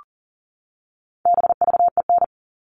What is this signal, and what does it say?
Morse code, heard as a keyed tone, about 40 wpm, 700 Hz. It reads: BVEN
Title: All Cut Numbers - 4-Digit 40wpm